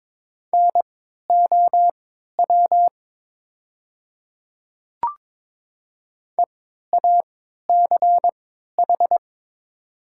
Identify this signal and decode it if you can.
Morse code, heard as a keyed tone, about 22 wpm, spaced as for 12 wpm, 700 Hz. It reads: NOW EACH
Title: 3x Character Spacing - Top 100 Words - 22wpm